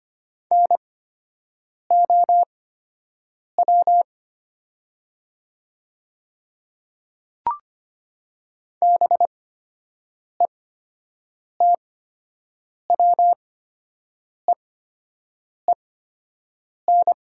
Morse code, heard as a keyed tone, about 25 wpm, spaced as for 7 wpm, 700 Hz. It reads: NOW BETWEEN
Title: 8x Character Spacing - Top 300 Words - 25wpm